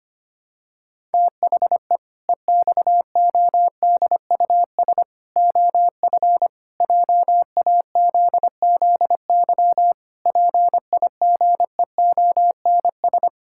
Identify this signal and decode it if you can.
Morse code, heard as a keyed tone, about 25 wpm, 700 Hz. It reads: THE EXODUS OF JAZZY PIGEONS
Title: Pangram Challenge v2 - 25wpm